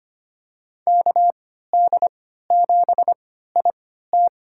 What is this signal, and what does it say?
Morse code, heard as a keyed tone, about 25 wpm, spaced as for 14 wpm, 700 Hz. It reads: KD7IT
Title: USA Call Signs - 3x Character Spacing - 25wpm